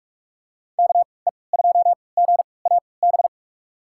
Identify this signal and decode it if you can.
Morse code, heard as a keyed tone, about 44 wpm, spaced as for 25 wpm, 700 Hz. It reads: KE2GAB
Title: USA Call Signs - 6 Characters - 3x Character Spacing - 45wpm